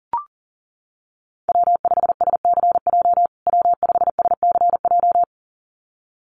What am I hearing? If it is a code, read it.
Morse code, heard as a keyed tone, about 40 wpm, 700 Hz. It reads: W5SCJ W5SCJ
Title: USA Call Signs Repeated 2x 40wpm